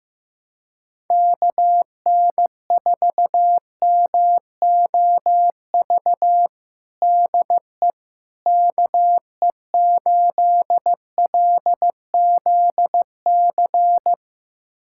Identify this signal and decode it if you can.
Morse code, heard as a keyed tone, about 15 wpm, 700 Hz. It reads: KN4MOV DE KE8LZC
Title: USA Callsign DE Callsign - 6 Characters - 15wpm